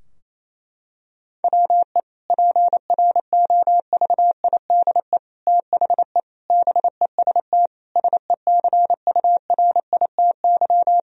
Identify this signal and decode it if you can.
Morse code, heard as a keyed tone, about 28 wpm, 700 Hz. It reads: WE PROVIDE THE BEST SECURITY
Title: Sentences from Top 1000 Words - Review Part 4 - 28wpm